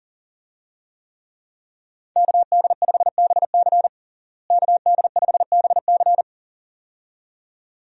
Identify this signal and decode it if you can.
Morse code, heard as a keyed tone, about 40 wpm, 700 Hz. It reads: KD5BC KD5BC
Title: USA Call Signs - 5 Characters - Repeated 2x - 40wpm